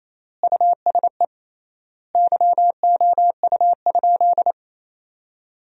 Morse code, heard as a keyed tone, about 28 wpm, 700 Hz. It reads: USE YOU?
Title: 3x Word-Spacing - Sentences from Top 200 Words - 28wpm